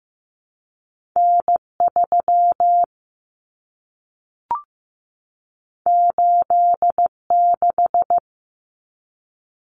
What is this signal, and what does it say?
Morse code, heard as a keyed tone, about 15 wpm, 700 Hz. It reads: N3 86
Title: Common Cut Numbers - 2-Digit 15wpm